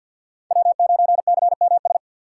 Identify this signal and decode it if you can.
Morse code, heard as a keyed tone, about 53 wpm, 700 Hz. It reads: W9CMS
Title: USA Call Signs - 5 Characters - 6x Character Spacing - 50wpm